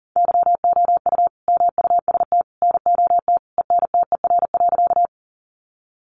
Morse code, heard as a keyed tone, about 40 wpm, 700 Hz. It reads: YOU MUST NOT ENTER.
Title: Sentences from Top 1300 Words 40wpm